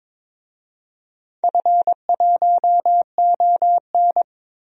Morse code, heard as a keyed tone, about 21 wpm, 700 Hz. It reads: F1ON
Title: DXCC-100 Call Signs - 4 Characters - 22wpm